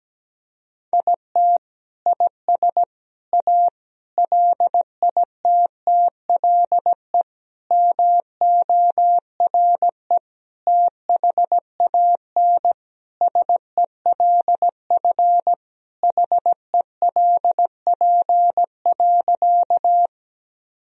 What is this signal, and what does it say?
Morse code, heard as a keyed tone, about 17 wpm, 700 Hz. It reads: IT IS A LITTLE MORE THAN SELF HELP.
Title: Sentences from Top 1000 Words - Review Part 1 - 17wpm